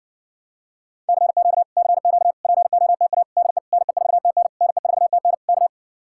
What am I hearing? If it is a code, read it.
Morse code, heard as a keyed tone, about 60 wpm, 700 Hz. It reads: CQ CQ POTA DE NE4TN NE4TN K